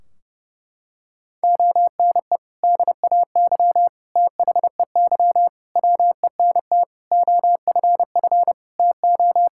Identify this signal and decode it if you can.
Morse code, heard as a keyed tone, about 30 wpm, 700 Hz. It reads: ONE DAY THEY WENT OFF TO
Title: Sentences from Top 1600 Words 30wpm